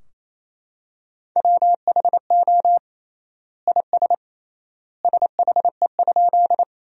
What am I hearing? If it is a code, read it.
Morse code, heard as a keyed tone, about 28 wpm, 700 Hz. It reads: WHO IS SHE?
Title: 3x Word-Spacing - Sentences from Top 100 Words - 28wpm